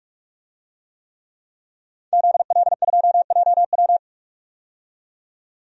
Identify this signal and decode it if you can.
Morse code, heard as a keyed tone, about 42 wpm, 700 Hz. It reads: ZL2JW
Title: DXCC-100 Call Signs - 5 Characters - 45wpm